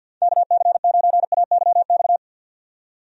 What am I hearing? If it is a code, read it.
Morse code, heard as a keyed tone, about 50 wpm, 700 Hz. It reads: KC9AYX
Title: USA Call Signs - 6 Characters - 50wpm